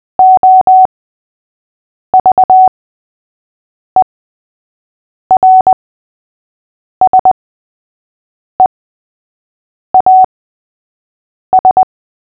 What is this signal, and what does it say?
Morse code, heard as a keyed tone, about 20 wpm, spaced as for 6 wpm, 750 Hz. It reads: OVERSEAS
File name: cw / fists / Short-Story-3-6wpm